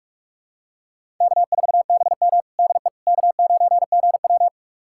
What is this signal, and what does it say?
Morse code, heard as a keyed tone, about 46 wpm, 700 Hz. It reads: K4BM DE K9GW
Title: USA Callsign DE Callsign - 4 Characters - 45wpm